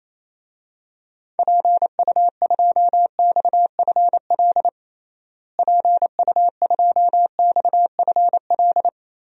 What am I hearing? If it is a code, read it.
Morse code, heard as a keyed tone, about 28 wpm, 700 Hz. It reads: PU2XFL PU2XFL
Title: DXCC-100 Call Signs - 6 Characters - Repeated 2x - 28wpm